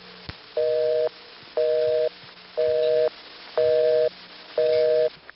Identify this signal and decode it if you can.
Busy tone (North America), 480 + 620 Hz.
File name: phonetrips / RingBusy01HQ